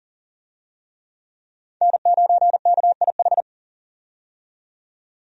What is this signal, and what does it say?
Morse code, heard as a keyed tone, about 40 wpm, 700 Hz. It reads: N9KIH
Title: USA Call Signs - 5 Characters - 40wpm